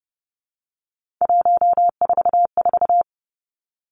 Morse code, heard as a keyed tone, about 30 wpm, 700 Hz. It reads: 144
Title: Three Challenging Letter-Number Combinations 30wpm